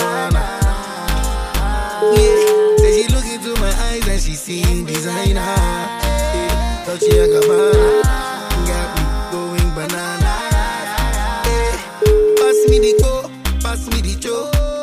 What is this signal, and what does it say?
Ringback tone (Europe, 425 Hz), with noise.